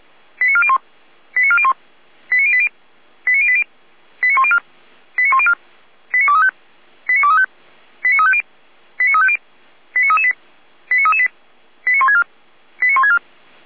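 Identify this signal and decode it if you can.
Five-tone selective call (ZVEI), heard as quick runs of five tones, tones of about 70 ms. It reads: 89401 89401 89080 89080 89104 89104 89235 89235 89350 89350 89338 89338 89164 89164
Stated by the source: ZVEI five-tone selective call received on an SDR.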